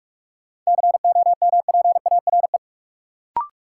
Morse code, heard as a keyed tone, about 46 wpm, 700 Hz. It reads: COMPARE
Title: Sentences from Top 1300 Words - New Word Drill - 45wpm